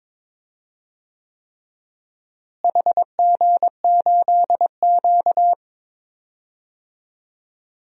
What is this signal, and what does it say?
Morse code, heard as a keyed tone, about 22 wpm, 700 Hz. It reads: HG8Q
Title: DXCC-100 Call Signs - 4 Characters - Repeated 3x - 22wpm